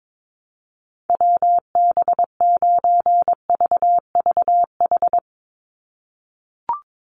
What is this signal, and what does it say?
Morse code, heard as a keyed tone, about 22 wpm, 700 Hz. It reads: WB9VVH
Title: USA Call Signs 22wpm